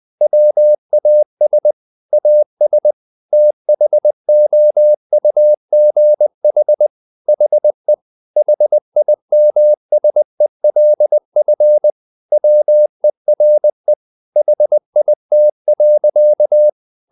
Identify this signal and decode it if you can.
Morse code, heard as a keyed tone, about 20 wpm, 600 Hz. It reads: WAS AS THOUGH HE HIMSELF WERE HIT.
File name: the old man and the sea part 2 20wpm